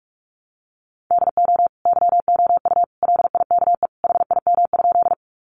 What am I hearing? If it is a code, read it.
Morse code, heard as a keyed tone, about 44 wpm, 700 Hz. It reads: DO YOU LIKE HIM?
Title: Sentences from Top 100 Words 45wpm